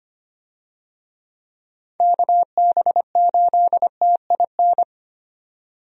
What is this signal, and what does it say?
Morse code, heard as a keyed tone, about 25 wpm, 700 Hz. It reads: KB8TIN
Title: USA Call Signs - 6 Characters - 25wpm